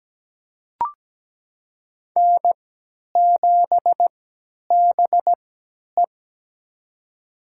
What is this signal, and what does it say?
Morse code, heard as a keyed tone, about 17 wpm, spaced as for 10 wpm, 700 Hz. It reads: N7BE
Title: USA Call Signs - 4 Characters - 3x Character Spacing - 17wpm